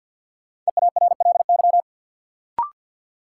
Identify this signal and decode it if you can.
Morse code, heard as a keyed tone, about 52 wpm, 700 Hz. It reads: EARLY
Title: Sentences from Top 400 Words - New Word Drill - 50wpm